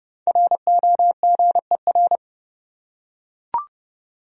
Morse code, heard as a keyed tone, about 31 wpm, 700 Hz. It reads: ROGER
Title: USA Common Names - Max 5 Letters - 30wpm